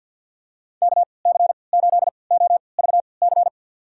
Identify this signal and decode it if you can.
Morse code, heard as a keyed tone, about 47 wpm, spaced as for 28 wpm, 700 Hz. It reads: KC8OVC
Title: USA Call Signs - 6 Characters - 3x Character Spacing - 50wpm